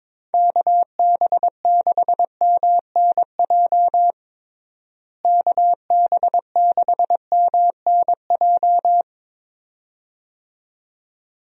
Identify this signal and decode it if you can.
Morse code, heard as a keyed tone, about 22 wpm, 700 Hz. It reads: KB6MNJ KB6MNJ
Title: USA Call Signs - 6 Characters - Repeated 2x - 22wpm